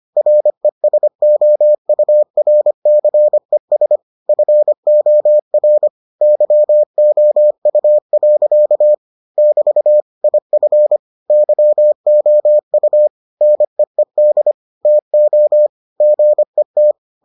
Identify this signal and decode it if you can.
Morse code, heard as a keyed tone, about 25 wpm, 600 Hz. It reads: RESOURCES FOR YOU. <BT> IF YOU NEED TO GET